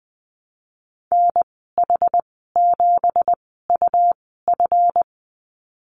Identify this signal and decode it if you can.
Morse code, heard as a keyed tone, about 20 wpm, spaced as for 14 wpm, 700 Hz. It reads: NH7UF
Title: 179 Lesson 29g - Slash - Call Signs 20wpm